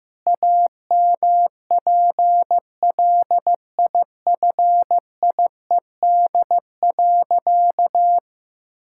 Morse code, heard as a keyed tone, about 15 wpm, 700 Hz. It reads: AMPLIFIED.
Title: Pangram Challenge v2 - 15wpm